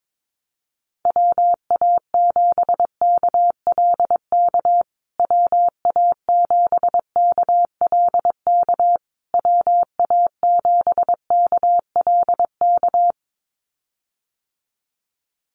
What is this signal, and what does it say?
Morse code, heard as a keyed tone, about 22 wpm, 700 Hz. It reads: WA7KLK WA7KLK WA7KLK
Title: USA Call Signs Repeated 3x 22wpm